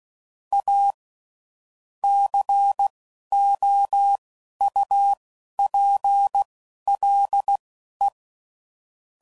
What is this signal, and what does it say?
Morse code, heard as a keyed tone, about 16 wpm, spaced as for 12 wpm, 800 Hz. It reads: A COUPLE
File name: CW Code Groups @10WPM / Technical Subjects / SwPwrSup